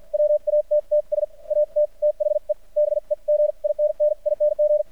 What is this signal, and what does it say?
Morse code, heard as a keyed tone, about 53 wpm, spaced as for 43 wpm, 600 Hz. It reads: WATTS AT THE BEGINNING
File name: archived / CRSnet